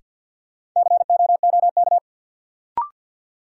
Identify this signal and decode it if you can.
Morse code, heard as a keyed tone, about 47 wpm, 700 Hz. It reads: COOK
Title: Sentences from Top 700 Words - New Word Drill - 50wpm